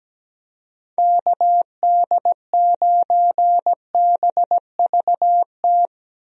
Morse code, heard as a keyed tone, about 17 wpm, 700 Hz. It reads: KD9BVT
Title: USA Call Signs - 6 Characters - Repeated 2x - 17wpm